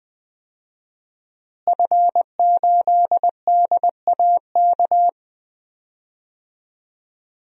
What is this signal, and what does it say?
Morse code, heard as a keyed tone, about 20 wpm, 700 Hz. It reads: F8DAK
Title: DXCC-100 Call Signs - 5 Characters - Repeated 3x - 20wpm